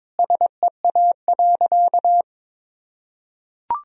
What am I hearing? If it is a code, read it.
Morse code, heard as a keyed tone, about 22 wpm, 700 Hz. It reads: SEA.
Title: Sentences from Top 1000 Words - Review Part 2 - 22wpm